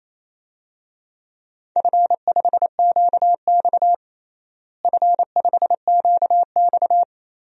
Morse code, heard as a keyed tone, about 28 wpm, 700 Hz. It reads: F5QX F5QX
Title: DXCC-100 Call Signs - 4 Characters - Repeated 2x - 28wpm